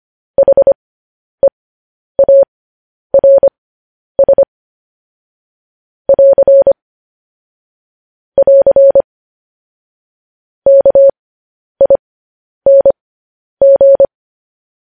Morse code, heard as a keyed tone, about 25 wpm, spaced as for 10 wpm, 550 Hz. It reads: HEARS <AR> <AR> KING
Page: News Headlines in Morse Code at 10 WPM